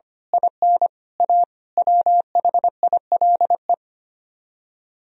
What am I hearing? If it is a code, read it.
Morse code, heard as a keyed tone, about 24 wpm, 700 Hz. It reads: IN A WHILE
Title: Speed-Racing - Sets of 5 Words 25wpm